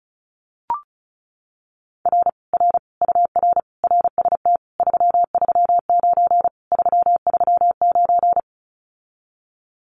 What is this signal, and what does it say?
Morse code, heard as a keyed tone, about 34 wpm, 700 Hz. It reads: R R UR RST 339 339
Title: Common QSO Phrases Encore 35wpm